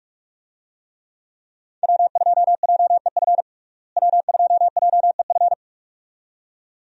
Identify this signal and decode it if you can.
Morse code, heard as a keyed tone, about 46 wpm, 700 Hz. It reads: W2JEF W2JEF
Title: USA Call Signs - 5 Characters - Repeated 2x - 45wpm